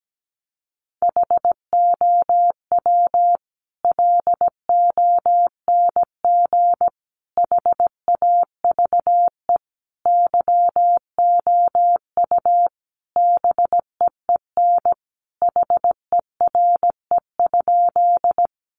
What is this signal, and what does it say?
Morse code, heard as a keyed tone, about 17 wpm, 700 Hz. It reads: HOW LONG HAVE YOU BEEN HERE?
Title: Sentences from Top 200 Words 17wpm